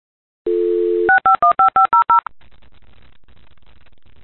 DTMF, heard as telephone touch tones, tones of about 120 ms. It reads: 6515500
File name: modem / modem28a